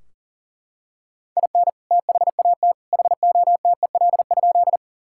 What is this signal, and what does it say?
Morse code, heard as a keyed tone, about 40 wpm, 700 Hz. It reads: IN THAT HOTEL?
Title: Sentences from Top 1600 Words 40wpm